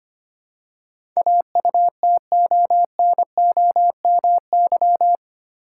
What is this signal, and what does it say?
Morse code, heard as a keyed tone, about 25 wpm, 700 Hz. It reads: AUTONOMY
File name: Eight Letter Words - 25wpm